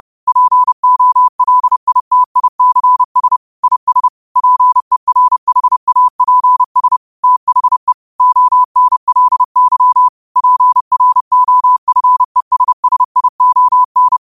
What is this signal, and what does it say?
Morse code, heard as a keyed tone, about 30 wpm, 1000 Hz. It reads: POLITICS IS PERHAPS THE ONLY PROFESSION
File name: qotdfiles / 30wpm